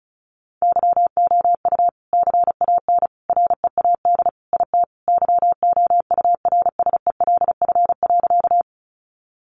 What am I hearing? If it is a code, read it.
Morse code, heard as a keyed tone, about 34 wpm, 700 Hz. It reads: YOU CAN READ IT YOURSELF.
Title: Sentences from Top 600 Words 35wpm